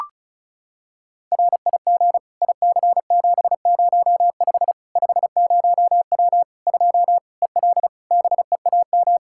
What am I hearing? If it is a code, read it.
Morse code, heard as a keyed tone, about 34 wpm, 700 Hz. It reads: RIG IC705 50W 2 EL BEAM
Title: Practice QSO - Exchange 2 - Info - 35wpm